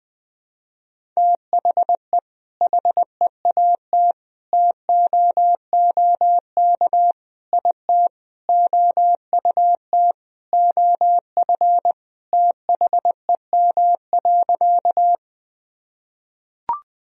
Morse code, heard as a keyed tone, about 20 wpm, 700 Hz. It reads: THE HEAT TOOK IT OUT OF THEM.
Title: Sentences from Top 1000 Words - Review Part 3 - 20wpm